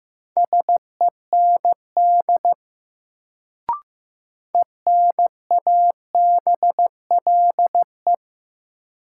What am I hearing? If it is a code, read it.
Morse code, heard as a keyed tone, about 15 wpm, 700 Hz. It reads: SEND ENABLE
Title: Sentences from Top 1500 Words - New Word Drill Alt - 15wpm